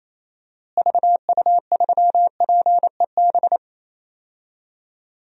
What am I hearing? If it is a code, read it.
Morse code, heard as a keyed tone, about 28 wpm, 700 Hz. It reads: VU3PEB
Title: DXCC-100 Call Signs - 6 Characters - Repeated 3x - 28wpm